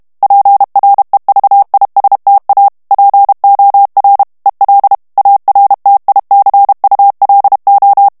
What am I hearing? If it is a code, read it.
Morse code, heard as a keyed tone, about 32 wpm, 800 Hz. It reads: PREVISTA POR EL ARTICULO
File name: ESPAGNOL TEXTE EN CLAIR 2 VITESSE 30 MOTS MINUTES